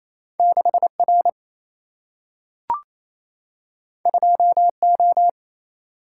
Morse code, heard as a keyed tone, about 28 wpm, 700 Hz. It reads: 6 R 2 O
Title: Single-Letter Single-Number 28wpm